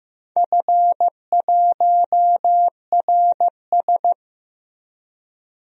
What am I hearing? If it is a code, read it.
Morse code, heard as a keyed tone, about 15 wpm, 700 Hz. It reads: F1RS
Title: DXCC-100 Call Signs - 4 Characters - Repeated 2x - 15wpm